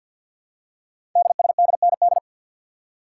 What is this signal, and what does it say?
Morse code, heard as a keyed tone, about 46 wpm, 700 Hz. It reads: DSDND
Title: Code Groups - 5 Letters - 50wpm